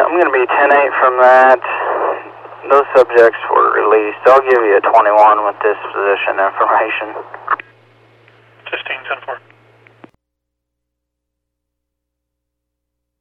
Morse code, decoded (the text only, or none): none